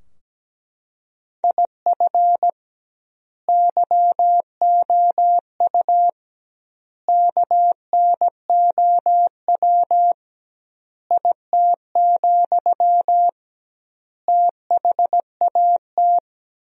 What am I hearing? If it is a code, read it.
Morse code, heard as a keyed tone, about 17 wpm, 700 Hz. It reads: IF YOU KNOW IT, THAT
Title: 2x Word-Spacing - Sentences from Top 400 Words - 17wpm